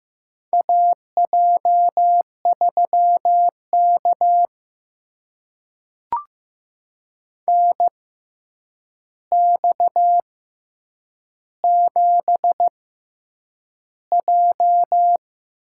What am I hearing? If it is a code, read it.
Morse code, heard as a keyed tone, about 15 wpm, spaced as for 8 wpm, 700 Hz. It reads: AJ3K NX7J
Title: USA Call Signs - 4 Characters - 6x Character Spacing - 15wpm